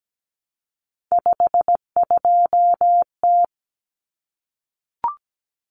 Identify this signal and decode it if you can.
Morse code, heard as a keyed tone, about 17 wpm, 700 Hz. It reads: 52T
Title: Common Cut Numbers - 3-Digit 17wpm